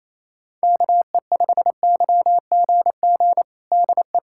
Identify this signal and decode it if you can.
Morse code, heard as a keyed tone, about 28 wpm, 700 Hz. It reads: KE5YGG DE
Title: USA Callsign DE Callsign - 6 Characters - 28wpm